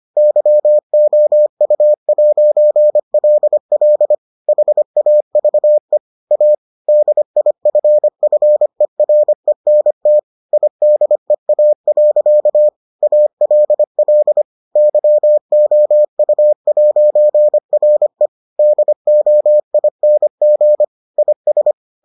Morse code, heard as a keyed tone, about 25 wpm, 600 Hz. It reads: YOU'LL HAVE A DIFFERENT IDEA. ALL YOU'RE DOING IS